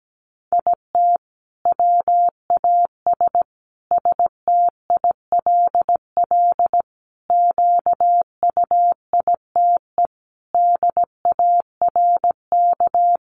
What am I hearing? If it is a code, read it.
Morse code, heard as a keyed tone, about 17 wpm, 700 Hz. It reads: IT WAS STILL QUITE DARK
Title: Sentences from Top 900 Words 17wpm